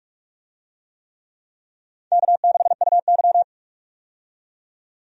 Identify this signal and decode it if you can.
Morse code, heard as a keyed tone, about 45 wpm, 700 Hz. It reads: K6UY
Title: USA Call Signs - 4 Characters - 45wpm